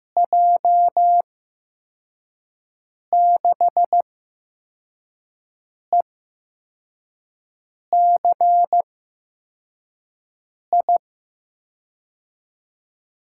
Morse code, heard as a keyed tone, about 15 wpm, spaced as for 4 wpm, 700 Hz. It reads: J6ECI
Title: USA Call Signs - 6 Characters - 8x Character Spacing - 15wpm